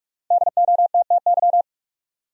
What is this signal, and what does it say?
Morse code, heard as a keyed tone, about 44 wpm, 700 Hz. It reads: DOTTY
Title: USA Common Names - Max 5 Letters - 45wpm